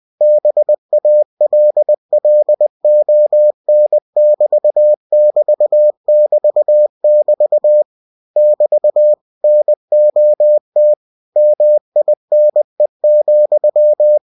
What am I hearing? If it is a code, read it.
Morse code, heard as a keyed tone, about 20 wpm, 600 Hz. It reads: BALLON<BT><BT><BT><BT> <BT> NOT MINE,
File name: cw-books / winnie-the-pooh / winnie-20wpm-page-0058